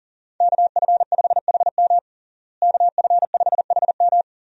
Morse code, heard as a keyed tone, about 40 wpm, 700 Hz. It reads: KF5HM KF5HM
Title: USA Call Signs - 5 Characters - Repeated 3x - 40wpm